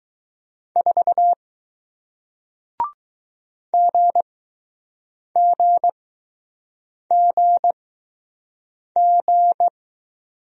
Morse code, heard as a keyed tone, about 20 wpm, spaced as for 6 wpm, 700 Hz. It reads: 4 GGGG